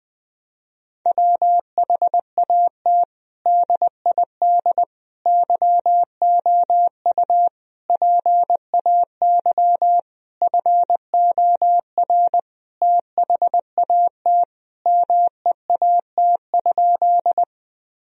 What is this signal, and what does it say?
Morse code, heard as a keyed tone, about 20 wpm, 700 Hz. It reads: WHAT DID YOU PAY FOR THAT MEAT?
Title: Sentences from Top 1000 Words - Review Part 3 - 20wpm